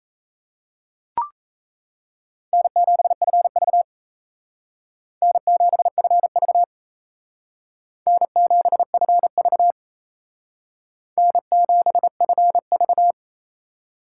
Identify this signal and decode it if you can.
Morse code, heard as a keyed tone, about 36 wpm, 700 Hz. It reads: N7FV N7FV N7FV N7FV